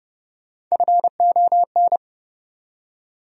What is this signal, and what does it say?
Morse code, heard as a keyed tone, about 31 wpm, 700 Hz. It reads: FON
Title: USA Call Sign Postfixes - 3 Letters 30wpm